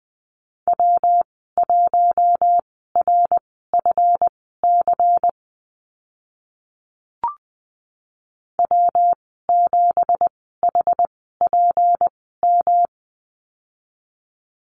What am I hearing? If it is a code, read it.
Morse code, heard as a keyed tone, about 20 wpm, spaced as for 14 wpm, 700 Hz. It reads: W1RFC W7HPM
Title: 167 Lesson 27g - Number 7 - Call Signs 20wpm